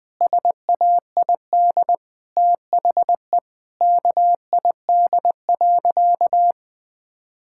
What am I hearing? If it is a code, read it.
Morse code, heard as a keyed tone, about 20 wpm, 700 Hz. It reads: SAID THE KID.
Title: Sentences from Top 1600 Words 20wpm